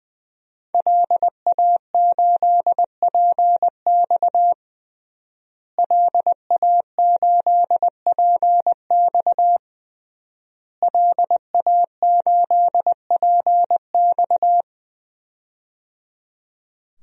Morse code, heard as a keyed tone, about 20 wpm, 700 Hz. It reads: LA8PX LA8PX LA8PX
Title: DXCC-100 Call Signs - 5 Characters - Repeated 3x - 20wpm